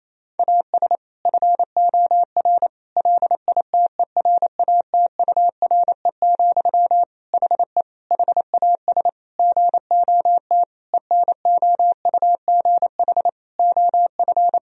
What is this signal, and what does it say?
Morse code, heard as a keyed tone, about 28 wpm, 700 Hz. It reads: AS FOR LITERATURE, HE HAS GOT ENOUGH OF
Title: Sentences from Top 1000 Words - Review Part 3 - 28wpm